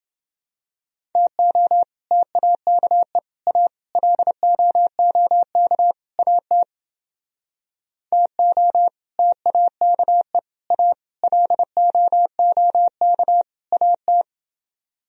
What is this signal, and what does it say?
Morse code, heard as a keyed tone, about 28 wpm, 700 Hz. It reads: TO TAKE A LOOK AT TO TAKE A LOOK AT